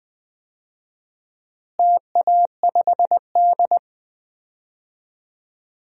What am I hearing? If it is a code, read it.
Morse code, heard as a keyed tone, about 20 wpm, 700 Hz. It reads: TA5D
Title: DXCC-100 Call Signs - 4 Characters - Repeated 2x - 20wpm